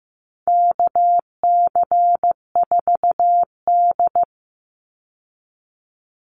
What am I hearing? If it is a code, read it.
Morse code, heard as a keyed tone, about 15 wpm, 700 Hz. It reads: KC4D
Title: Call Signs Repeated 3x - 15wpm